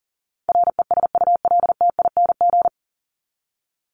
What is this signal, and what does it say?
Morse code, heard as a keyed tone, about 40 wpm, 700 Hz. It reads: RESULTING
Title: ING Suffix - Top 100 Words with Base Words 40wpm